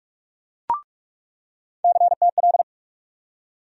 Morse code, heard as a keyed tone, about 41 wpm, 700 Hz. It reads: CTL
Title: USA Call Sign Postfixes - 3 Letters 45wpm